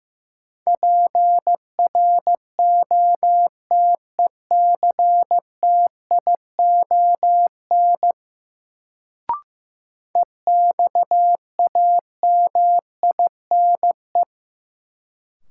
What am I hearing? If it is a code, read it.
Morse code, heard as a keyed tone, about 15 wpm, 700 Hz. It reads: PROTECTION EXAMINE
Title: Sentences from Top 1600 Words - New Word Drill - 15wpm